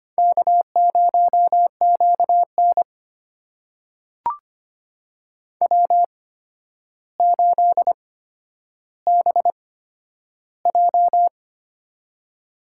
Morse code, heard as a keyed tone, about 25 wpm, spaced as for 11 wpm, 700 Hz. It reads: K0QN W8BJ
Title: USA Call Signs - 4 Characters - 8x Character Spacing - 25wpm